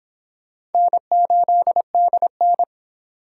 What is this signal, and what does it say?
Morse code, heard as a keyed tone, about 30 wpm, 700 Hz. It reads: N8DN